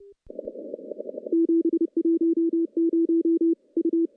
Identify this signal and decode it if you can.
Morse code, heard as a keyed tone, about 30 wpm, 330 Hz. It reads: 710 U
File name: contests / CW-SS-k5tr- -1200